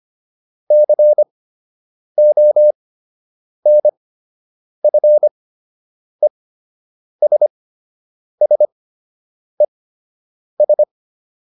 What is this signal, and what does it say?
Morse code, heard as a keyed tone, about 25 wpm, spaced as for 8 wpm, 600 Hz. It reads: CONFESSES